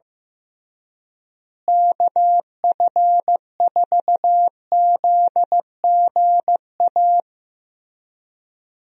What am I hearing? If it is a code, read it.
Morse code, heard as a keyed tone, about 15 wpm, 700 Hz. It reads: KF4ZGA
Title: USA Call Signs - 6 Characters - Repeated 3x - 15wpm